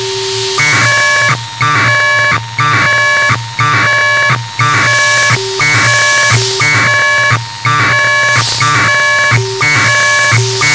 Packet radio (AX.25, as in APRS).
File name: Sohla-1 23h48 utc 1200bd packet NormalModeby ja0caw